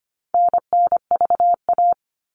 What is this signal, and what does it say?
Morse code, heard as a keyed tone, about 25 wpm, 700 Hz. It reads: NNVA
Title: All Cut Numbers - 4-Digit 25wpm